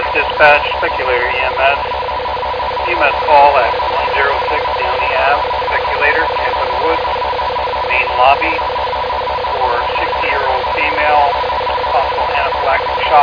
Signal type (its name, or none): two-tone teleprinter signal